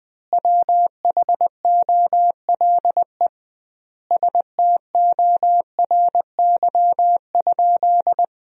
Morse code, heard as a keyed tone, about 20 wpm, 700 Hz. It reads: WHOLE STORY?
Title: 2x Word-Spacing - Sentences from Top 400 Words - 20wpm